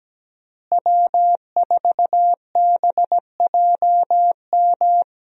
Morse code, heard as a keyed tone, about 17 wpm, 700 Hz. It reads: W4BJM
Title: USA Call Signs - 5 Characters - Repeated 3x - 17wpm